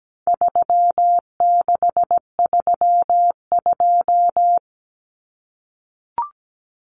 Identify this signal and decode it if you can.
Morse code, heard as a keyed tone, about 17 wpm, 700 Hz. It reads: 3632
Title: Common Cut Numbers - 4-Digit 17wpm